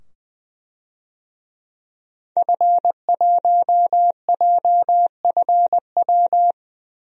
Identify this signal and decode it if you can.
Morse code, heard as a keyed tone, about 20 wpm, 700 Hz. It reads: F1JFW
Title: DXCC-100 Call Signs - 5 Characters - 20wpm